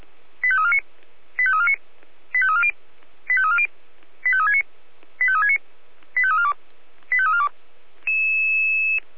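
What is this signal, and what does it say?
Five-tone selective call (ZVEI), heard as quick runs of five tones, tones of about 70 ms. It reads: 85349 85349 85340 85340 85369 85369 85342 85342
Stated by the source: ZVEI five-tone selective call received on an SDR.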